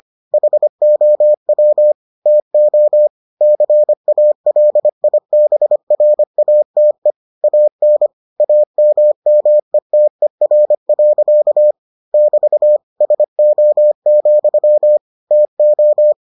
Morse code, heard as a keyed tone, about 25 wpm, 600 Hz. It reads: HOW TO CALIBRATE AN AMMETER. <BT> SO, TO